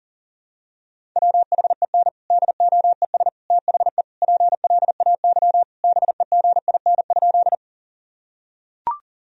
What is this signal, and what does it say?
Morse code, heard as a keyed tone, about 40 wpm, 700 Hz. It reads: WHEN DOES THE PLAY BEGIN?
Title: Sentences from Top 1000 Words - Review Part 1 - 40wpm